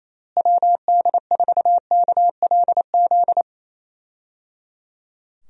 Morse code, heard as a keyed tone, about 28 wpm, 700 Hz. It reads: WD4KLZ
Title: USA Call Signs - 6 Characters - Repeated 3x - 28wpm